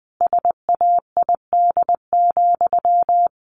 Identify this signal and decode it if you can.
Morse code, heard as a keyed tone, about 20 wpm, 700 Hz. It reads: SAID,
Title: Sentences from Top 1200 Words 20wpm